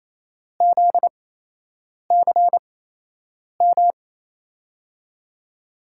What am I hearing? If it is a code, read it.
Morse code, heard as a keyed tone, about 28 wpm, spaced as for 8 wpm, 700 Hz. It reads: ZCM
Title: USA Call Signs - 5 Characters - 8x Character Spacing - 28wpm